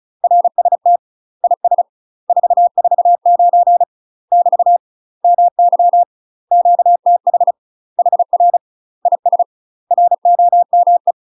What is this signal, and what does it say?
Morse code, heard as a keyed tone, about 35 wpm, 700 Hz. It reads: RST IS 449 <BT> MY QTH HR IS ROME
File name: UK MASTER QSOs TRAINING / 35WPM / UK MASTER QSOs 0037